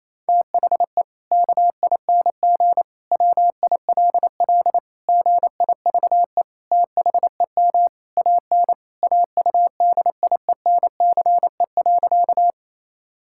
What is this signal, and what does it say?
Morse code, heard as a keyed tone, about 27 wpm, 700 Hz. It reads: THE KING WILL GIVE THEM AN AUDIENCE.
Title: Sentences from Top 1000 Words - Review Part 5 - 28wpm